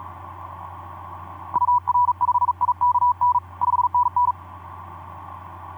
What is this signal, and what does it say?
Morse code, heard as a keyed tone, about 40 wpm, 980 Hz. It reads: UR5IKN 5NN
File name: d44ac ek2rgt etc 10m cw 23nov2010 1202